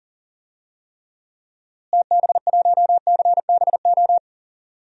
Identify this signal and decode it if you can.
Morse code, heard as a keyed tone, about 40 wpm, 700 Hz. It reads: TB1CBO
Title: DXCC-100 Call Signs - 6 Characters - Repeated 3x - 40wpm